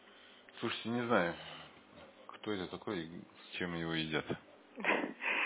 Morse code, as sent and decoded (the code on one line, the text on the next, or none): none